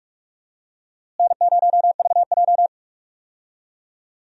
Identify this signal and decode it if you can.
Morse code, heard as a keyed tone, about 46 wpm, 700 Hz. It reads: N0VJ
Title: USA Call Signs - 4 Characters - Repeated 2x - 45wpm